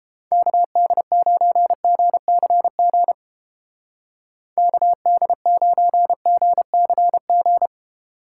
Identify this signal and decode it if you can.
Morse code, heard as a keyed tone, about 31 wpm, 700 Hz. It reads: KD9GCG KD9GCG